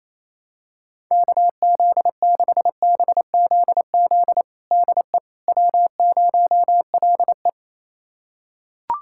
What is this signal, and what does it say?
Morse code, heard as a keyed tone, about 28 wpm, 700 Hz. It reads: KZ6BZZ DE W0LE
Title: USA Callsign DE Callsign 28wpm